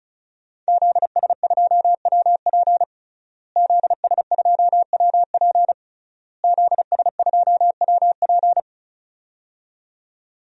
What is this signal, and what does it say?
Morse code, heard as a keyed tone, about 36 wpm, 700 Hz. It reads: ZS2WP ZS2WP ZS2WP
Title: DXCC-100 Call Signs - 5 Characters - Repeated 3x - 35wpm